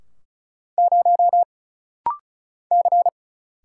Morse code, heard as a keyed tone, about 36 wpm, 700 Hz. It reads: 0 C
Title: Single Letter-Number Alt - Rapid-Fire 35wpm